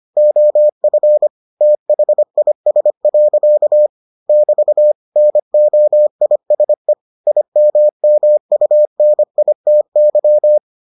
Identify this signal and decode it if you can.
Morse code, heard as a keyed tone, about 25 wpm, 600 Hz. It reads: OF THIS. <BT> NOISE IMMUNITY